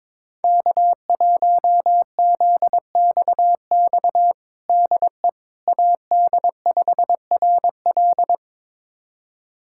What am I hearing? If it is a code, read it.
Morse code, heard as a keyed tone, about 22 wpm, 700 Hz. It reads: K1ZXX DE AD5RL
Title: USA Callsign DE Callsign - 5 Characters - 22wpm